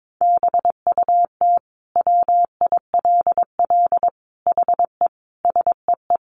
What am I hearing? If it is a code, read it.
Morse code, heard as a keyed tone, about 22 wpm, 700 Hz. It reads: BUT WILL HE SEE
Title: Sentences from Top 100 Words 22wpm